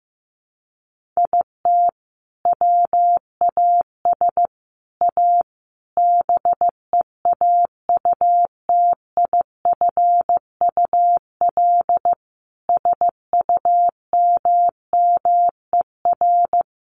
Morse code, heard as a keyed tone, about 15 wpm, 700 Hz. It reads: IT WAS A BEAUTIFUL SUMMER
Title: Sentences from Top 1100 Words 15wpm